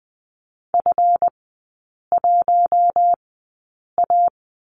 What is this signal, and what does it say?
Morse code, heard as a keyed tone, about 20 wpm, spaced as for 8 wpm, 700 Hz. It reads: F1A
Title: 166 Lesson 27e - Number 7 - Three Characters 20wpm